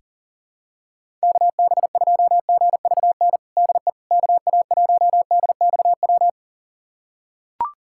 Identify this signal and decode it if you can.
Morse code, heard as a keyed tone, about 40 wpm, 700 Hz. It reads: KB2GVN DE KA1DXW
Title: USA Callsign DE Callsign - 6 Characters - 40wpm